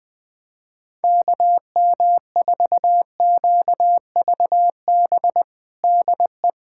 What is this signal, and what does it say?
Morse code, heard as a keyed tone, about 20 wpm, 700 Hz. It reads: KM4QVB DE
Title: USA Callsign DE Callsign - 6 Characters - 20wpm